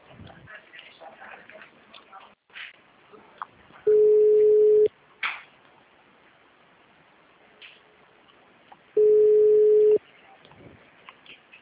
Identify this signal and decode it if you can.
Ringback tone (Europe, 425 Hz).
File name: out- -2021- - - .83470